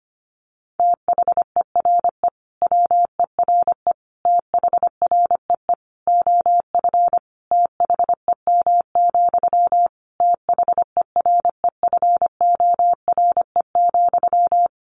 Morse code, heard as a keyed tone, about 25 wpm, 700 Hz. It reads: THERE WERE THREE OF THEM, THEREFORE,
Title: Sentences from Top 800 Words 25wpm